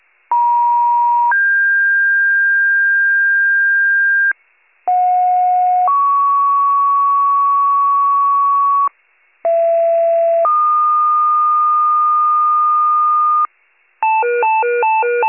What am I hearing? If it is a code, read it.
Two-tone page, heard as a one-second tone, then a three-second tone, about 950 then 1600 Hz, about 730 then 1120 Hz, about 670 then 1220 Hz. It is heard from 0.3 to 13.5 s.